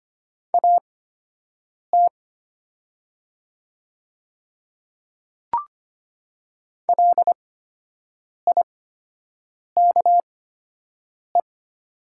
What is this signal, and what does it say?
Morse code, heard as a keyed tone, about 25 wpm, spaced as for 7 wpm, 700 Hz. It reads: AT LIKE
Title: 8x Character Spacing - Top 100 Words - 25wpm